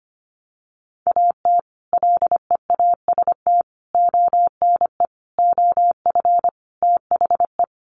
Morse code, heard as a keyed tone, about 25 wpm, 700 Hz. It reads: AT LEAST ONE OF THE
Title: Sets of 5 Words Encore 25wpm